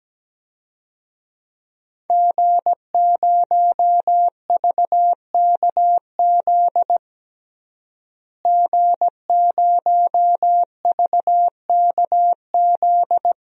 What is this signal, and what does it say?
Morse code, heard as a keyed tone, about 17 wpm, 700 Hz. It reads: G0VKZ G0VKZ
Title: DXCC-100 Call Signs - 5 Characters - Repeated 2x - 17wpm